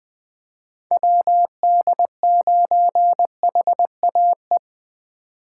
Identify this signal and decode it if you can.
Morse code, heard as a keyed tone, about 20 wpm, 700 Hz. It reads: WD9HAE